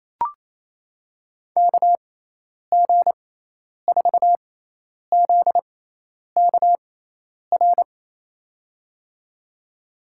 Morse code, heard as a keyed tone, about 27 wpm, spaced as for 10 wpm, 700 Hz. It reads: KG4ZKR
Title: USA Call Signs - 6x Character Spacing - 28wpm